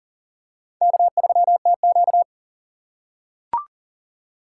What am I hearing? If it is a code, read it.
Morse code, heard as a keyed tone, about 40 wpm, 700 Hz. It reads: K3TQ
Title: USA Call Signs - 4 Characters - 6x Character Spacing - 40wpm